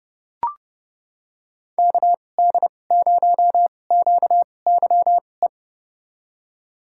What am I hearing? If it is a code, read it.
Morse code, heard as a keyed tone, about 30 wpm, spaced as for 22 wpm, 700 Hz. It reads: KD0QYE
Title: USA Call Signs - 6 Characters - 2x Character Spacing - 30wpm